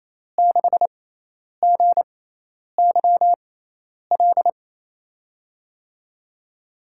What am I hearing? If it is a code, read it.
Morse code, heard as a keyed tone, about 28 wpm, spaced as for 10 wpm, 700 Hz. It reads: 6GYL
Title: USA Call Signs - 6x Character Spacing - 28wpm